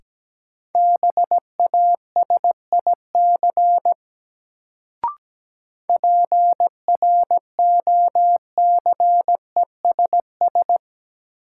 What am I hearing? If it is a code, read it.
Morse code, heard as a keyed tone, about 17 wpm, 700 Hz. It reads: BASIC PROCESS
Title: Sentences from Top 1000 Words - Review Part 3 - New Word Drill - 17wpm